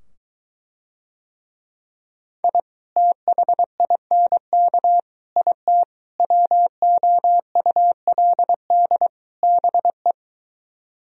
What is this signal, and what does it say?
Morse code, heard as a keyed tone, about 24 wpm, 700 Hz. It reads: I THINK IT WOULD BE